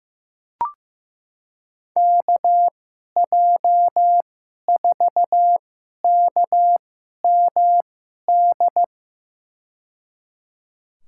Morse code, heard as a keyed tone, about 16 wpm, spaced as for 11 wpm, 700 Hz. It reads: KJ4KMD
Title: USA Call Signs - 2x Character Spacing - 15wpm